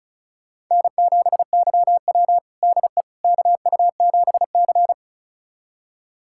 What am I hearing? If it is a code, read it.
Morse code, heard as a keyed tone, about 36 wpm, 700 Hz. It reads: N7YW DE KU7C
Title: USA Callsign DE Callsign - 4 Characters - 35wpm